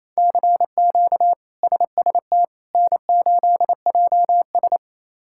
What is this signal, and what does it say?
Morse code, heard as a keyed tone, about 28 wpm, 700 Hz. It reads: CQ SST N8JS
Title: K1USN SST QSO Practice - 28wpm